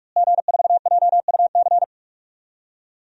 Morse code, heard as a keyed tone, about 46 wpm, 700 Hz. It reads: G4JUC
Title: DXCC-100 Call Signs - 5 Characters - Repeated 3x - 45wpm